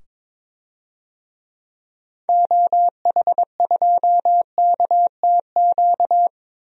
Morse code, heard as a keyed tone, about 22 wpm, 700 Hz. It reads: OH2KTQ
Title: DXCC-100 Call Signs - 6 Characters - 22wpm